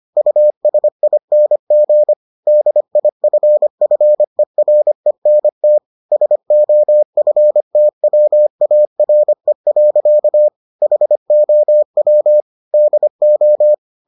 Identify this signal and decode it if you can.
Morse code, heard as a keyed tone, about 25 wpm, 600 Hz. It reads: USING DIFFERENT SOFTWARE. HOW DO